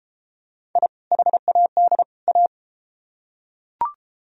Morse code, heard as a keyed tone, about 31 wpm, 700 Hz. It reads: I HAD A